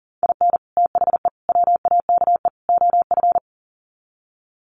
Morse code, heard as a keyed tone, about 40 wpm, 700 Hz. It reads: IN THE WAKE OF
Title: Sets of 4 Words 40wpm